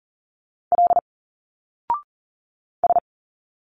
Morse code, heard as a keyed tone, about 40 wpm, 700 Hz. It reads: L S
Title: Single Letters 40wpm